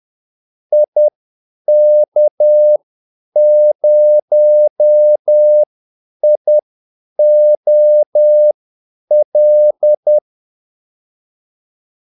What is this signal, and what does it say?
Morse code, heard as a keyed tone, about 10 wpm, spaced as for 8 wpm, 600 Hz. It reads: IK0IOL
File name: Callsigns-10-8wpm